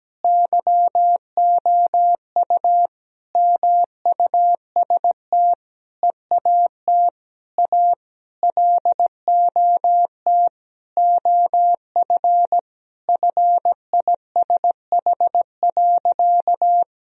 Morse code, heard as a keyed tone, about 17 wpm, 700 Hz. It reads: YOU MUST EAT A LOT OF FISH.
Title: Sentences from Top 1000 Words - Review Part 3 - 17wpm